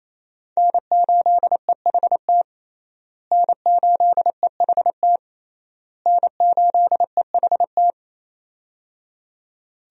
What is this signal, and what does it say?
Morse code, heard as a keyed tone, about 28 wpm, 700 Hz. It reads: N8EHT N8EHT N8EHT
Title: USA Call Signs - 5 Characters - Repeated 3x - 28wpm